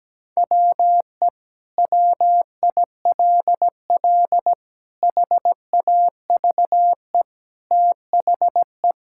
Morse code, heard as a keyed tone, about 17 wpm, 700 Hz. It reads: WE WILL HAVE THE
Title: Sentences from Top 1000 Words - Review Part 6 - 17wpm